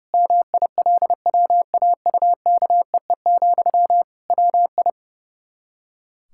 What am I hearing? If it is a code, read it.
Morse code, heard as a keyed tone, about 32 wpm, 700 Hz. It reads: MILWAUKEE, WI
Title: K1USN SST QSO Practice 30wpm